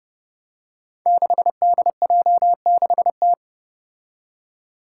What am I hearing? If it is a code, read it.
Morse code, heard as a keyed tone, about 32 wpm, 700 Hz. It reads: 6DJ6T
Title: Five Challenging Letter-Number Combinations 30wpm